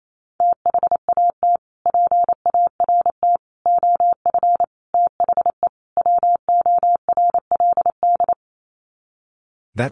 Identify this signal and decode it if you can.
Morse code, heard as a keyed tone, about 28 wpm, 700 Hz. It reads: THAT PART OF THE WORLD
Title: Sets of 5 Words Encore2 28wpm